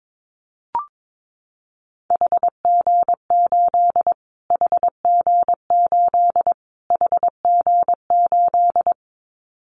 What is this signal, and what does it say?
Morse code, heard as a keyed tone, about 22 wpm, 700 Hz. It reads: HG8 HG8 HG8
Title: Callsign Prefixes for Top 25 DXCC - Repeated 3x - 22wpm